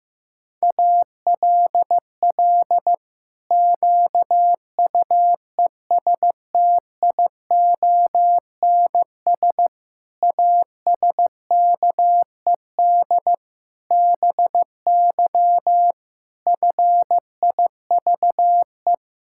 Morse code, heard as a keyed tone, about 15 wpm, 700 Hz. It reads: ALL QUESTIONS ASKED BY FIVE
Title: Pangram Challenge v2 - 15wpm